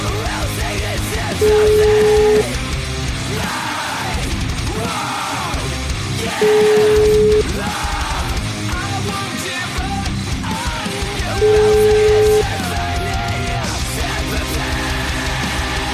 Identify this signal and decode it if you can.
Ringback tone (Europe, 425 Hz), with noise.